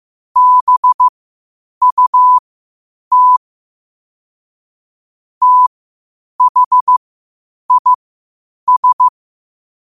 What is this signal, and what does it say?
Morse code, heard as a keyed tone, about 15 wpm, spaced as for 8 wpm, 1000 Hz. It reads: BUT THIS
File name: qotdfiles / 8wpm